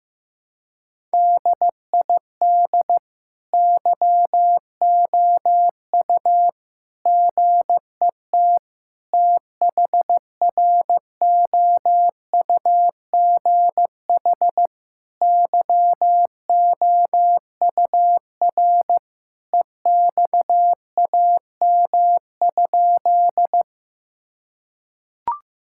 Morse code, heard as a keyed tone, about 15 wpm, 700 Hz. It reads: DID YOU GET THROUGH YOUR EXAM?
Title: Sentences from Top 1000 Words - Review Part 5 - 15wpm